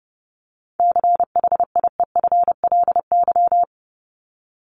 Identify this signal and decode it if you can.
Morse code, heard as a keyed tone, about 30 wpm, 700 Hz. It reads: CHIEFLY
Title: LY Suffix - Top 100 Words with Base Words 30wpm